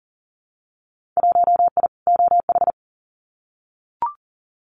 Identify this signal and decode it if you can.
Morse code, heard as a keyed tone, about 41 wpm, 700 Hz. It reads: 1I OH
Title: Winter Field Day Exchanges 40wpm